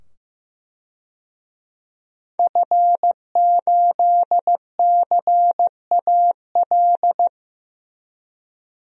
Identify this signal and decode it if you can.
Morse code, heard as a keyed tone, about 15 wpm, 700 Hz. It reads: F8CAL
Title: DXCC-100 Call Signs - 5 Characters - 15wpm